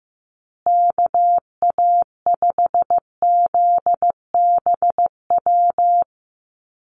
Morse code, heard as a keyed tone, about 15 wpm, 700 Hz. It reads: KA5ZBW
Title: USA Call Signs Repeated 2x 15wpm